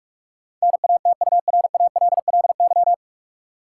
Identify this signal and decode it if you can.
Morse code, heard as a keyed tone, about 46 wpm, 700 Hz. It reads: NATURALLY
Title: Sentences from Top 1300 Words - New Word Drill Alt - 45wpm